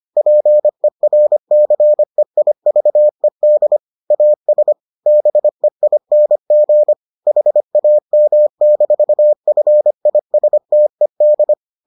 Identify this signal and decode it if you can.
Morse code, heard as a keyed tone, about 25 wpm, 600 Hz. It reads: PERCEIVED AS BEING HAM-FISTED